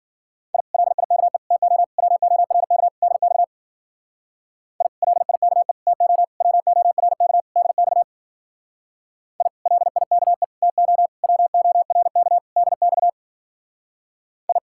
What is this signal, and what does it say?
Morse code, heard as a keyed tone, about 48 wpm, 700 Hz. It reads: I LIKE TO WORK DX I LIKE TO WORK DX I LIKE TO WORK DX I
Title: Speed-Racing Common QSO Phrases - 40wpm